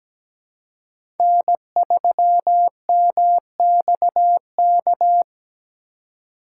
Morse code, heard as a keyed tone, about 17 wpm, 700 Hz. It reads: N3MXK
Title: USA Call Signs - 5 Characters - 2x Character Spacing - 17wpm